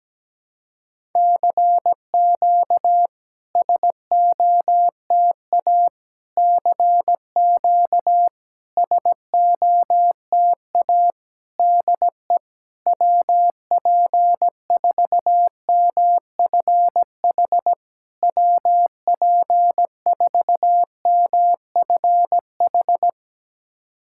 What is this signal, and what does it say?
Morse code, heard as a keyed tone, about 17 wpm, 690 Hz. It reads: CQ SOTA CQ SOTA DE WP4MFH WP4MFH
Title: SOTA QSO Practice 17wpm